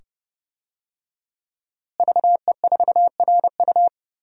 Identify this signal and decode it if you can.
Morse code, heard as a keyed tone, about 30 wpm, 700 Hz. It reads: VE4RU
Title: DXCC-100 Call Signs - 5 Characters - Repeated 2x - 30wpm